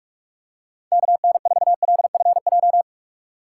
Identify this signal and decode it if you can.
Morse code, heard as a keyed tone, about 45 wpm, 700 Hz. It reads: KN4LFJ
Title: USA Call Signs - 6 Characters - 3x Character Spacing - 45wpm